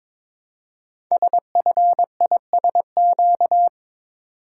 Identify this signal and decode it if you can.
Morse code, heard as a keyed tone, about 23 wpm, 700 Hz. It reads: SFISQ
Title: Code Groups - 5 Letter:Numbers - 22wpm